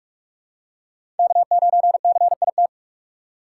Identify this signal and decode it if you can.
Morse code, heard as a keyed tone, about 48 wpm, 700 Hz. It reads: K9CIT
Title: USA Call Signs - 6x Character Spacing - 45wpm